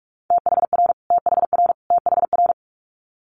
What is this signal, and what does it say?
Morse code, heard as a keyed tone, about 46 wpm, 700 Hz. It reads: THR THR THR
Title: Multi-letter Phonemes Repeated 3x 45wpm